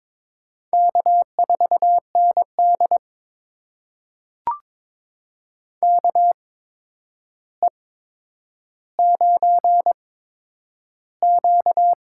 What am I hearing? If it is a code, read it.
Morse code, heard as a keyed tone, about 22 wpm, spaced as for 9 wpm, 700 Hz. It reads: K4ND KE9Q
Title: USA Call Signs - 4 Characters - 8x Character Spacing - 22wpm